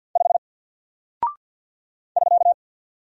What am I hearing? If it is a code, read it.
Morse code, heard as a keyed tone, about 50 wpm, 700 Hz. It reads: 5 <SK>
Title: Single Character (Letter:Number:Punctuation:Prosign) 50wpm